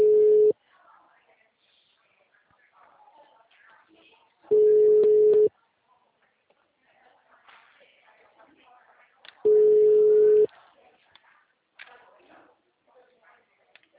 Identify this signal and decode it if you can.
Ringback tone (Europe, 425 Hz), with speech.